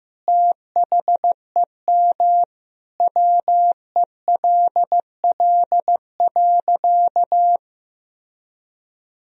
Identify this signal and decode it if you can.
Morse code, heard as a keyed tone, about 15 wpm, 700 Hz. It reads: THEM WELL.
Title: Sentences from Top 1000 Words - Review Part 1 - 15wpm